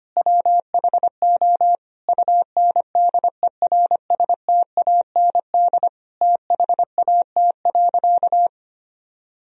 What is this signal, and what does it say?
Morse code, heard as a keyed tone, about 25 wpm, 700 Hz. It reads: WHO UNDERSTAND THAT.
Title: Sentences from Top 1000 Words - Review Part 1 - 25wpm